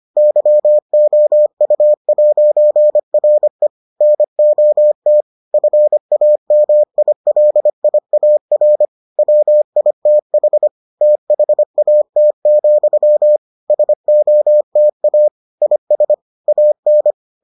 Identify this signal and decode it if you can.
Morse code, heard as a keyed tone, about 25 wpm, 600 Hz. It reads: YOU'RE NOT FAMILIAR WITH THAT, SOTA IS AN